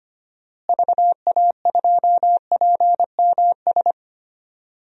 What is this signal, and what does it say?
Morse code, heard as a keyed tone, about 25 wpm, 700 Hz. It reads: VA2PMS
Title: DXCC-100 Call Signs - 6 Characters - Repeated 2x - 25wpm